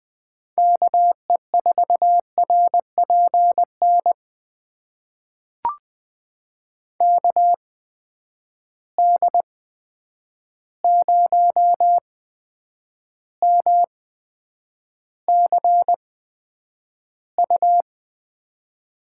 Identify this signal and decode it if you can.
Morse code, heard as a keyed tone, about 20 wpm, spaced as for 9 wpm, 700 Hz. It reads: KE4RPN KD0MCU
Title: USA Call Signs - 6 Characters - 8x Character Spacing - 20wpm